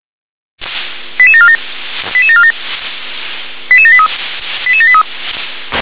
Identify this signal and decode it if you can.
Five-tone selective call (ZVEI), heard as quick runs of five tones, tones of about 70 ms. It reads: 88747 88747 88783 88783
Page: /357/SDR/ZVEI/88631/